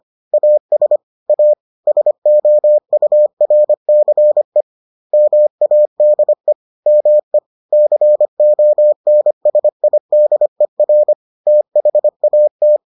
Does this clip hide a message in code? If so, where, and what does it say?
Yes: Morse code from 0.3 to 12.8 s, heard as a keyed tone, about 25 wpm, 600 Hz. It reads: AS A SOURCE MADE ME CONSIDER THAT